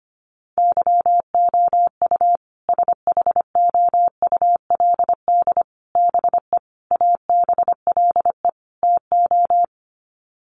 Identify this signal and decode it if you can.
Morse code, heard as a keyed tone, about 25 wpm, 700 Hz. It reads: YOU SHOULD BE ABLE TO
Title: Sets of 5 Words Encore 25wpm